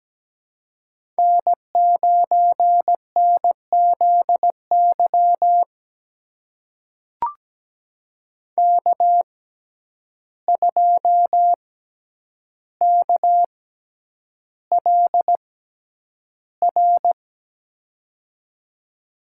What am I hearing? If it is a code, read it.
Morse code, heard as a keyed tone, about 17 wpm, spaced as for 9 wpm, 700 Hz. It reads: N9NZY K2KLR
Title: USA Call Signs - 5 Characters - 6x Character Spacing - 17wpm